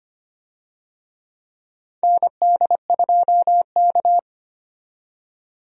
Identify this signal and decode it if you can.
Morse code, heard as a keyed tone, about 25 wpm, 700 Hz. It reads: ND2K
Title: USA Call Signs - 4 Characters - 25wpm